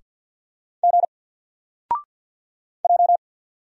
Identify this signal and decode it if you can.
Morse code, heard as a keyed tone, about 50 wpm, 700 Hz. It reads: G J
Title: Single Character (Letter:Number:Punctuation:Prosign) 50wpm